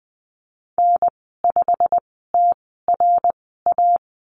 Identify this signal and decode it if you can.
Morse code, heard as a keyed tone, about 20 wpm, spaced as for 15 wpm, 700 Hz. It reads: N5TRA
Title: 128 Lesson 21g - Number 6 - Call Signs 20wpm